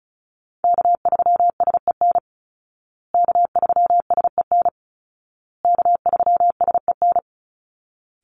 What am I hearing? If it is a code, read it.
Morse code, heard as a keyed tone, about 34 wpm, 700 Hz. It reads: K3SEN K3SEN K3SEN
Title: Call Signs Repeated 2x - 35wpm